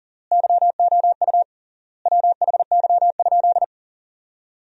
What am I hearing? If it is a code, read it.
Morse code, heard as a keyed tone, about 37 wpm, 700 Hz. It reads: YOU WHY?
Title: 3x Word-Spacing - Sentences from Top 200 Words - 40wpm